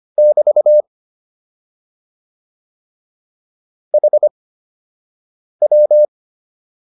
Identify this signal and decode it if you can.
Morse code, heard as a keyed tone, about 25 wpm, spaced as for 6 wpm, 600 Hz. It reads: <BT> HW